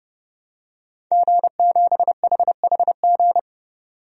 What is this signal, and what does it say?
Morse code, heard as a keyed tone, about 32 wpm, 700 Hz. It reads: G7HHG
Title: DXCC-100 Call Signs - 5 Characters - 30wpm